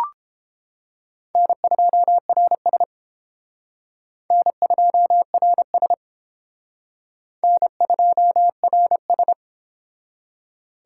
Morse code, heard as a keyed tone, about 31 wpm, 700 Hz. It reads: N2RS N2RS N2RS